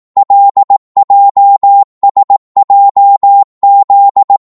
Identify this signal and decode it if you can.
Morse code, heard as a keyed tone, about 18 wpm, 800 Hz. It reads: LJSJZ
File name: Morse / ltr-18wpm